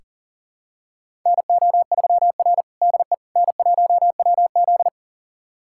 Morse code, heard as a keyed tone, about 40 wpm, 700 Hz. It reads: NO3R DE N1WZ
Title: USA Callsign DE Callsign - 4 Characters - 40wpm